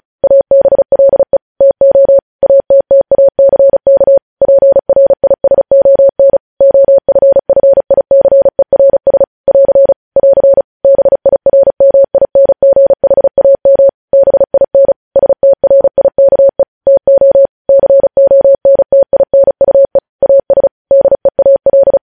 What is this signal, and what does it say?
Morse code, heard as a keyed tone, about 36 wpm, 550 Hz. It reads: ABLE TO ATTACK PRISON OFFICERS <AR> <AR> BIRMINGHAM BIN STRIKE TO CONTINUE AS DEAL
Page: news headlines in morse code at 35 wpm